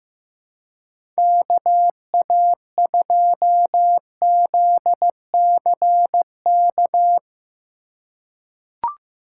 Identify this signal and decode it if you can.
Morse code, heard as a keyed tone, about 15 wpm, 700 Hz. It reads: KA2ZCK
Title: USA Call Signs - 6 Characters - 2x Character Spacing - 15wpm